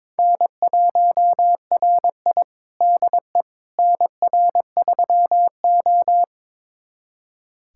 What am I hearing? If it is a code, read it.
Morse code, heard as a keyed tone, about 22 wpm, 700 Hz. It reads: N1RI DE NR3O
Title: USA Callsign DE Callsign - 4 Characters - 22wpm